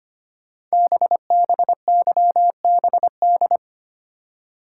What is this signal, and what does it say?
Morse code, heard as a keyed tone, about 25 wpm, 700 Hz. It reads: BBYBD
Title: Code Groups - 5 Letters - 25wpm